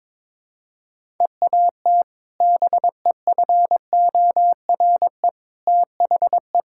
Morse code, heard as a keyed tone, about 23 wpm, 700 Hz. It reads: EAT BEFORE THE
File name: practice / 22wpm / Sentences with 2-Word Phrases - 22wpm